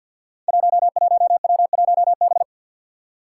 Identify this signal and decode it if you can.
Morse code, heard as a keyed tone, about 51 wpm, 700 Hz. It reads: 11W1B
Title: Five Challenging Letter-Number Combinations 50wpm